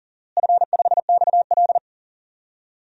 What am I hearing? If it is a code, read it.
Morse code, heard as a keyed tone, about 40 wpm, 700 Hz. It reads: F5XL
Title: DXCC-100 Call Signs - 4 Characters - Repeated 3x - 40wpm